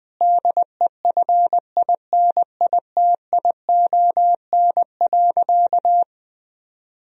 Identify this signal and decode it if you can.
Morse code, heard as a keyed tone, about 20 wpm, 700 Hz. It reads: DEFINITION.
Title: Sentences from Top 1000 Words - Review Part 4 - 20wpm